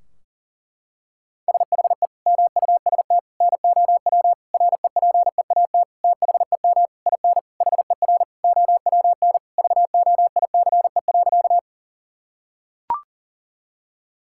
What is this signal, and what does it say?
Morse code, heard as a keyed tone, about 40 wpm, 700 Hz. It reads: SHE MUST NOW REPEAT THEM IN HER OWN VOICE.
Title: Sentences from Top 1600 Words 40wpm